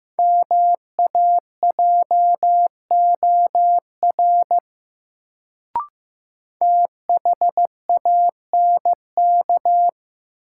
Morse code, heard as a keyed tone, about 15 wpm, 700 Hz. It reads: MAJOR THANK
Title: Sentences from Top 600 Words - New Word Drill - 15wpm